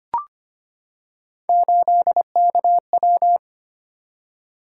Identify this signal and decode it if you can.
Morse code, heard as a keyed tone, about 25 wpm, 700 Hz. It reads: 8KW
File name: Three Letter-Number Combinations - 25wpm